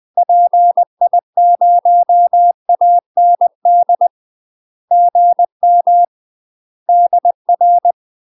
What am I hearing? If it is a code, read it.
Morse code, heard as a keyed tone, about 20 wpm, 700 Hz. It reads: PI0AND GM DR
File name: UK MASTER QSOs TRAINING / 20WPM / UK MASTER QSOs 0029